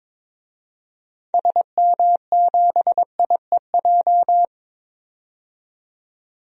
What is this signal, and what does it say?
Morse code, heard as a keyed tone, about 22 wpm, 700 Hz. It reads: SM7IEJ
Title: DXCC-100 Call Signs - 6 Characters - 22wpm